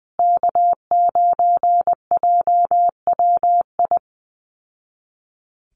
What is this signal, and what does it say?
Morse code, heard as a keyed tone, about 19 wpm, 700 Hz. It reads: K9JWI
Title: Call Signs Repeated 2x - 20wpm